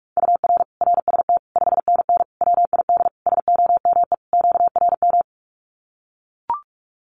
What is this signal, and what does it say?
Morse code, heard as a keyed tone, about 44 wpm, 700 Hz. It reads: UR RST 5NN WID SOME QRM
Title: Common QSO Phrases 45wpm